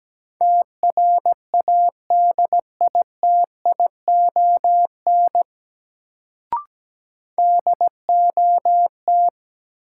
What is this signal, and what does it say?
Morse code, heard as a keyed tone, about 17 wpm, 700 Hz. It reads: TRADITION DOT
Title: Sentences from Top 1500 Words - New Word Drill Alt - 17wpm